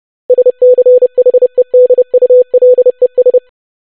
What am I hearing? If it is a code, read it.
Morse code, heard as a keyed tone, about 30 wpm, 500 Hz. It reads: SCHEDULES
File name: longwords-25wpm